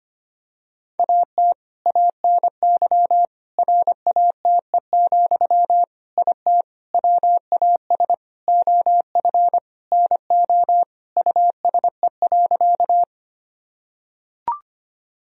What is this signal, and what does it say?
Morse code, heard as a keyed tone, about 25 wpm, 700 Hz. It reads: AT ANY RATE, IT WAS OF NO USE.
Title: Sentences from Top 1000 Words - Review Part 4 - 25wpm